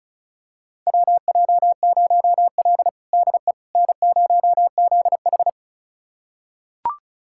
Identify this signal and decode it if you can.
Morse code, heard as a keyed tone, about 34 wpm, 700 Hz. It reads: WJ0L DE N0ZH
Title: USA Callsign DE Callsign - 4 Characters - 35wpm